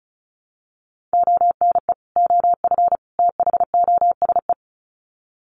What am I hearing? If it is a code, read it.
Morse code, heard as a keyed tone, about 34 wpm, 700 Hz. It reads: ONE OF THOSE
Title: Sets of 3 Words 35wpm